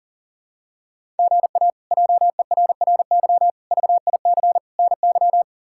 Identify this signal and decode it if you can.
Morse code, heard as a keyed tone, about 40 wpm, 700 Hz. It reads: GA JERRY VIC NY
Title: K1USN SST QSO Practice - 40wpm